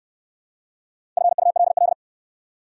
Morse code, heard as a keyed tone, about 60 wpm, 700 Hz. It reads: FULL